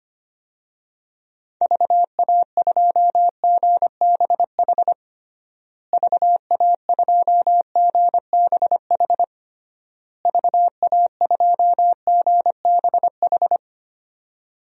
Morse code, heard as a keyed tone, about 25 wpm, 700 Hz. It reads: VA2GBH VA2GBH VA2GBH
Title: DXCC-100 Call Signs - 6 Characters - Repeated 3x - 25wpm